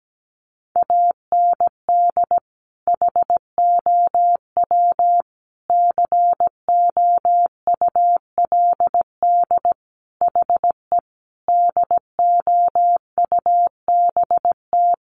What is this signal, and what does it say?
Morse code, heard as a keyed tone, about 17 wpm, 700 Hz. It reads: AND HOW COULD HE DOUBT
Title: Sentences from Top 1500 Words 17wpm